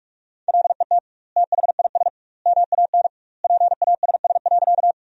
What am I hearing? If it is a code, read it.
Morse code, heard as a keyed tone, about 42 wpm, 700 Hz. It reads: LET THIS MAN PASS.
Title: 2x Word-Spacing - Sentences from Top 400 Words - 45wpm